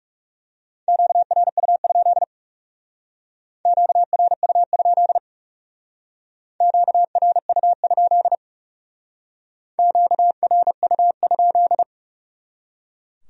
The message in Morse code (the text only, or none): QRU? QRU? QRU? QRU?